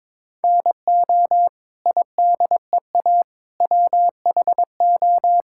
Morse code, heard as a keyed tone, about 22 wpm, 700 Hz. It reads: NO IDEA WHO
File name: Sets of 9 Words - 22wpm